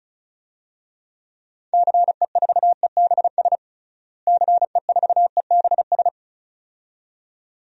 Morse code, heard as a keyed tone, about 34 wpm, 700 Hz. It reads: CE4EBS CE4EBS
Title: DXCC-100 Call Signs - 6 Characters - Repeated 2x - 35wpm